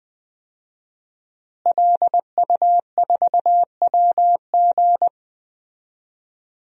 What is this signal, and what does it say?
Morse code, heard as a keyed tone, about 20 wpm, 700 Hz. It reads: LU4WG
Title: DXCC-100 Call Signs - 5 Characters - Repeated 3x - 20wpm